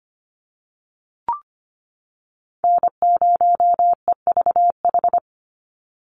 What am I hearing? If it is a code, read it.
Morse code, heard as a keyed tone, about 25 wpm, 700 Hz. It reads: N0EVH
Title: NA SOTA Chaser Callsigns 25wpm